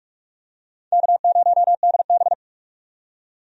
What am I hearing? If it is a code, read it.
Morse code, heard as a keyed tone, about 46 wpm, 700 Hz. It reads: K0DB
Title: USA Call Signs - 4 Characters - 45wpm